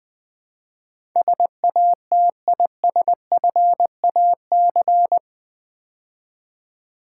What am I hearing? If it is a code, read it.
Morse code, heard as a keyed tone, about 20 wpm, 700 Hz. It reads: SATISFAC
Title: Build A Word - 4 Letters At A Time - 12-Letter Words - 20wpm